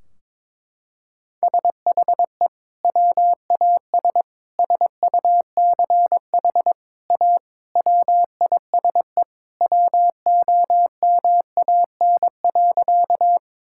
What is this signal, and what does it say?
Morse code, heard as a keyed tone, about 22 wpm, 700 Hz. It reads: SHE WAS SUCH A WISE WOMAN.
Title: Sentences from Top 1600 Words 22wpm